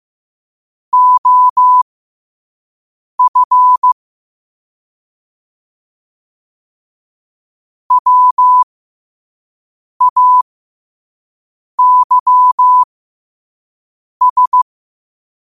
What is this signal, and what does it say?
Morse code, heard as a keyed tone, about 15 wpm, spaced as for 5 wpm, 1000 Hz. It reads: OF WAYS
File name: qotdfiles / 5wpm